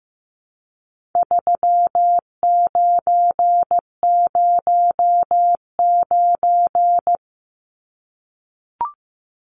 Morse code, heard as a keyed tone, about 15 wpm, 700 Hz. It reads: 3909
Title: Numbers - 4-Digit 15wpm